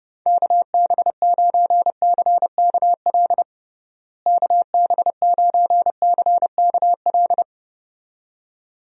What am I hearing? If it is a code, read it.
Morse code, heard as a keyed tone, about 31 wpm, 700 Hz. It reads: KB9CKL KB9CKL
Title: USA Call Signs - 6 Characters - Repeated 2x - 30wpm